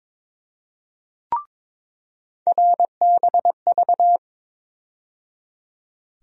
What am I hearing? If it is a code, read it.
Morse code, heard as a keyed tone, about 22 wpm, 700 Hz. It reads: RBV
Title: USA Call Sign Postfixes - 3 Letters 22wpm